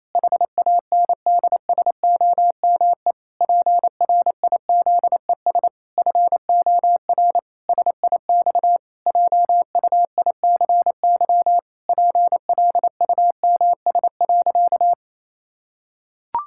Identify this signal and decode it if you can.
Morse code, heard as a keyed tone, about 28 wpm, 700 Hz. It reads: HANDSOME PRIZES FOR SIX JUICY PLUMS.
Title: Pangram Challenge v2 - 28wpm